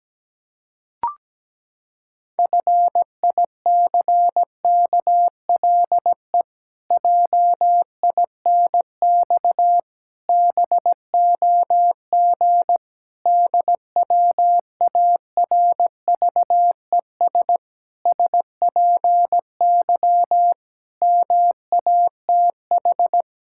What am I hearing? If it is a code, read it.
Morse code, heard as a keyed tone, about 17 wpm, 700 Hz. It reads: FICKLE JINX BOG DWARVES SPY MATH
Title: Pangram Challenge v2 - 17wpm